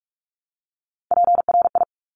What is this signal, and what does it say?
Morse code, heard as a keyed tone, about 46 wpm, 700 Hz. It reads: PRI
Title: Top Three Letter Combinations in Words 45wpm